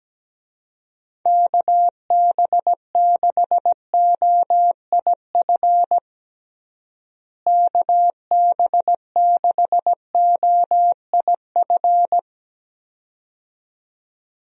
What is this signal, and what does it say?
Morse code, heard as a keyed tone, about 17 wpm, 700 Hz. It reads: KB6OIF KB6OIF
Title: USA Call Signs - 6 Characters - Repeated 2x - 17wpm